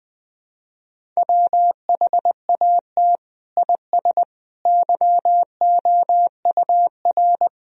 Morse code, heard as a keyed tone, about 20 wpm, 700 Hz. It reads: WHAT IS YOUR
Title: Sentences from Top 1000 Words - Review Part 1 - 20wpm